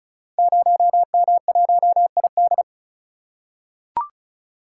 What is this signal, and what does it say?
Morse code, heard as a keyed tone, about 36 wpm, 700 Hz. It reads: 0M1ID
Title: Code Groups - 5 Letter:Numbers - 35wpm